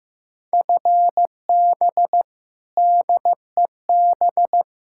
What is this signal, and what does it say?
Morse code, heard as a keyed tone, about 15 wpm, 700 Hz. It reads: FB DEB
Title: K1USN SST QSO Practice 15wpm